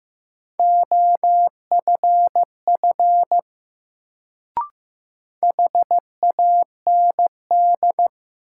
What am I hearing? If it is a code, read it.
Morse code, heard as a keyed tone, about 15 wpm, 700 Hz. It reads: OFF HAND
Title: Sentences from Top 200 Words - New Word Drill - 15wpm